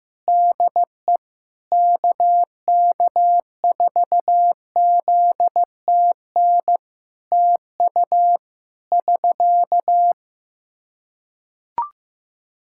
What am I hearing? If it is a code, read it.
Morse code, heard as a keyed tone, about 15 wpm, 700 Hz. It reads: DE KK4ZTN TU <SK>
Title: Practice QSO - Exchange 4 - Ending - 15wpm